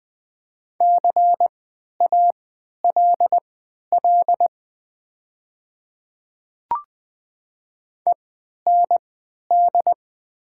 Morse code, heard as a keyed tone, about 20 wpm, spaced as for 11 wpm, 700 Hz. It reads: CALL END
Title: 3x Character Spacing - Top 200 Words - 20wpm